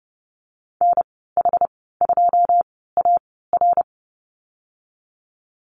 Morse code, heard as a keyed tone, about 31 wpm, spaced as for 17 wpm, 700 Hz. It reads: NH2AR
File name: Lesson 15g - Number 5 - Call Signs 30wpm